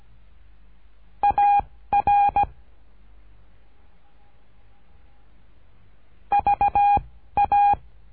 Morse code, heard as a keyed tone, about 17 wpm, 810 Hz. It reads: A R V A
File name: cursos / CURSO DE TELEGRAFIA / Telegrafia - Trilha 13